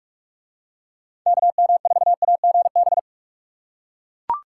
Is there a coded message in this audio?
Yes: Morse code from 1.3 to 3.0 s, heard as a keyed tone, about 43 wpm, 700 Hz. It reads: KM4AGB